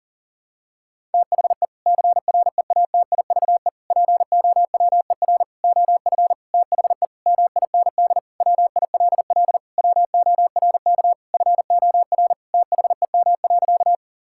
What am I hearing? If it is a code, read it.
Morse code, heard as a keyed tone, about 37 wpm, 700 Hz. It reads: THE CREATIVE POWER OF THE MIND WILL WORK FOR THEM.
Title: Sentences from Top 1000 Words - Review Part 5 - 40wpm